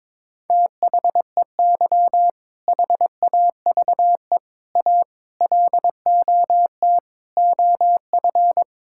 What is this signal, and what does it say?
Morse code, heard as a keyed tone, about 22 wpm, 700 Hz. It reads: THEY HAVE A LOT OF
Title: Sentences from Top 1000 Words - Review Part 5 - 22wpm